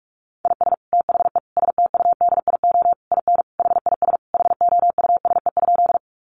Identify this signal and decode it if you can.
Morse code, heard as a keyed tone, about 44 wpm, 700 Hz. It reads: IS THE STUDIO IN HIS HOUSE?
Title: Sentences from Top 1200 Words 45wpm